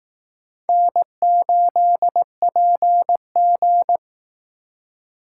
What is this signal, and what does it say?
Morse code, heard as a keyed tone, about 18 wpm, 700 Hz. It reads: N8PG
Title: Speed Racing - USA 1x2 Call Signs 15wpm